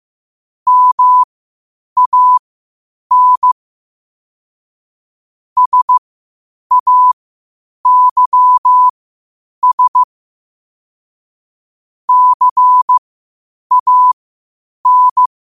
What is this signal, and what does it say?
Morse code, heard as a keyed tone, about 15 wpm, spaced as for 8 wpm, 1000 Hz. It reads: MAN SAYS CAN
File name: qotdfiles / 8wpm